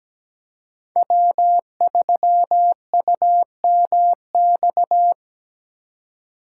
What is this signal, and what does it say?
Morse code, heard as a keyed tone, about 17 wpm, 700 Hz. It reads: W3UMX
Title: USA Call Signs - 5 Characters - 4x Character Spacing - 17wpm